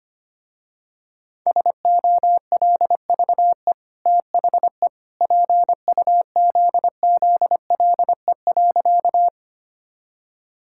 Morse code, heard as a keyed tone, about 25 wpm, 690 Hz. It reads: SOLVE THE PUZZLE.
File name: Easy Sentences with Hints - 25wpm